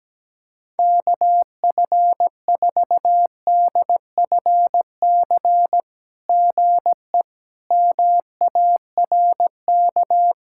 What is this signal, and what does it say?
Morse code, heard as a keyed tone, about 16 wpm, 700 Hz. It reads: KF4DFC GE MARK
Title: SOTA QSO Practice 17wpm